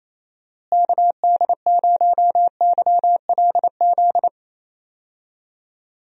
Morse code, heard as a keyed tone, about 28 wpm, 700 Hz. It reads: KD0YLZ
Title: USA Call Signs - 6 Characters - Repeated 2x - 28wpm